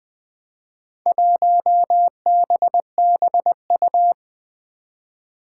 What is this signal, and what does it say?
Morse code, heard as a keyed tone, about 20 wpm, 700 Hz. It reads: 1BBU
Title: Four Challenging Letter-Number Combinations 20wpm